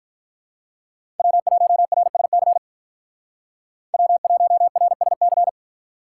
Morse code, heard as a keyed tone, about 48 wpm, 700 Hz. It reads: W1RSC W1RSC